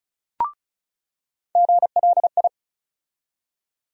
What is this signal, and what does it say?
Morse code, heard as a keyed tone, about 36 wpm, 700 Hz. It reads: GLI
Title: USA Call Sign Postfixes - 3 Letters 35wpm